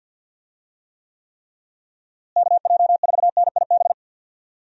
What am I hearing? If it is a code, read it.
Morse code, heard as a keyed tone, about 45 wpm, 700 Hz. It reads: KJ4NIB
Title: USA Call Signs - 6 Characters - Repeated 3x - 50wpm